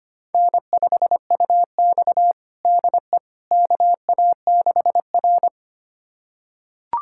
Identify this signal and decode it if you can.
Morse code, heard as a keyed tone, about 25 wpm, 700 Hz. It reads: N5UX DE KA6R
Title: USA Callsign DE Callsign - 4 Characters - 25wpm